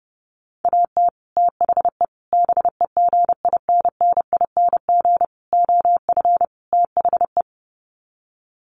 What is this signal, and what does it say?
Morse code, heard as a keyed tone, about 30 wpm, 700 Hz. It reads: AT THE BEGINNING OF THE
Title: Sets of 5 Words 30wpm